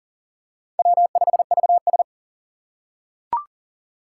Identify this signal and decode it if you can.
Morse code, heard as a keyed tone, about 40 wpm, 700 Hz. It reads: W5VS
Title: USA Call Signs - 4 Characters - 4x Character Spacing - 40wpm